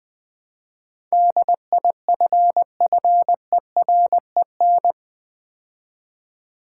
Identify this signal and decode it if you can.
Morse code, heard as a keyed tone, about 20 wpm, 700 Hz. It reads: DIFFEREN
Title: Build A Word - 4 Letters At A Time - 11-Letter Words - 20wpm